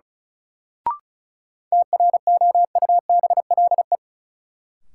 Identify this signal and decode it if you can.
Morse code, heard as a keyed tone, about 36 wpm, 700 Hz. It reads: TROUBLE
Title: Sentences from Top 1200 Words - New Word Drill Alt - 35wpm